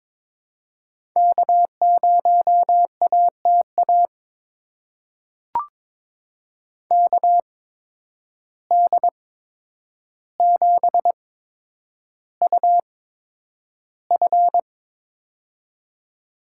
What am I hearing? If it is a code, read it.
Morse code, heard as a keyed tone, about 22 wpm, spaced as for 9 wpm, 700 Hz. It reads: K0ATA KD7UF
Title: USA Call Signs - 8x Character Spacing - 22wpm